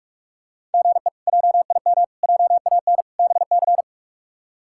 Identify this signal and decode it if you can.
Morse code, heard as a keyed tone, about 45 wpm, 700 Hz. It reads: GE JIM JAN BC
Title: K1USN SST QSO Practice - 45wpm